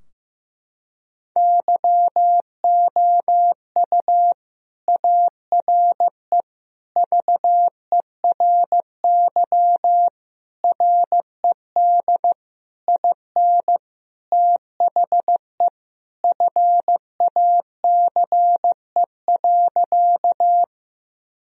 Morse code, heard as a keyed tone, about 15 wpm, 700 Hz. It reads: YOU ARE VERY RED IN THE FACE.
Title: Sentences from Top 1000 Words - Review Part 2 - 15wpm